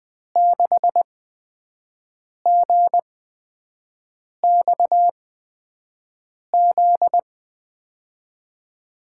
Morse code, heard as a keyed tone, about 20 wpm, spaced as for 5 wpm, 700 Hz. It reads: 6GXZ
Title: USA Call Signs - 6 Characters - 8x Character Spacing - 20wpm